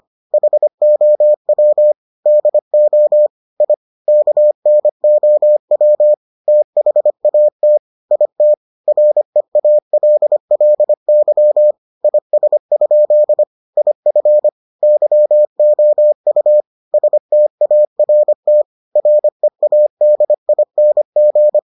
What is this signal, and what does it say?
Morse code, heard as a keyed tone, about 25 wpm, 600 Hz. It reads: HOW DO I KNOW THAT IT REALLY IS? IF YOU START READING